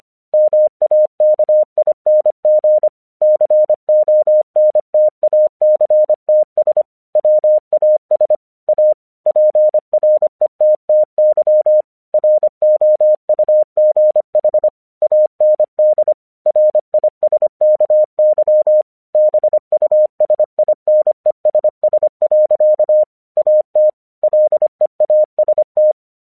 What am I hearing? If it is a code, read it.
Morse code, heard as a keyed tone, about 25 wpm, 610 Hz. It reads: MAKING CONTACTS WAS A PRETTY ROUGH AND RISKY BUSINESS. AT LEAST
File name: 1 English 25wpm / AQC32